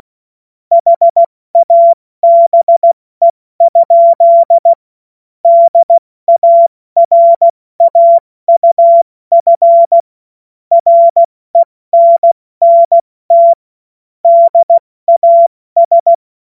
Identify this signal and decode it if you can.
Morse code, heard as a keyed tone, about 16 wpm, 680 Hz. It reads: HABE? DARAUF RENNT DAS
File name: Morsekurs / Witze / Witz 16. November 2021, 70 F80